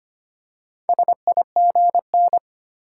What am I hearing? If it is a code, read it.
Morse code, heard as a keyed tone, about 25 wpm, 700 Hz. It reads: SIGN
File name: practice / 25wpm / Easy Words with Hints - 25wpm